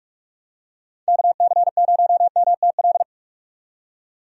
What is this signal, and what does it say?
Morse code, heard as a keyed tone, about 42 wpm, 700 Hz. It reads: KC0MTL
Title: USA Call Signs - 6 Characters - 45wpm